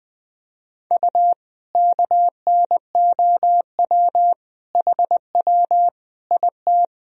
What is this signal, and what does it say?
Morse code, heard as a keyed tone, about 20 wpm, 700 Hz. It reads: U KNOW HW IT
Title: Sentences from Top 100 Words and Abbreviations 20wpm